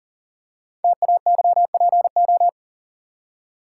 Morse code, heard as a keyed tone, about 40 wpm, 690 Hz. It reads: TAYPO
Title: Code Groups - 5 Letters - 40wpm